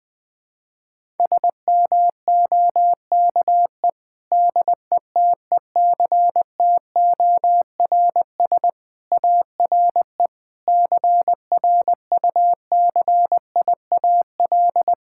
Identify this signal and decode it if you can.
Morse code, heard as a keyed tone, about 20 wpm, 700 Hz. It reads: SMOKE DETECTORS ARE CRUCIAL
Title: Difficult Sentences with Hints - 20wpm